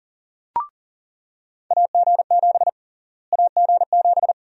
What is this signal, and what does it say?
Morse code, heard as a keyed tone, about 40 wpm, 700 Hz. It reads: AG7 AG7
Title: USA Call Sign Prefixes - Repeated 2x - 40wpm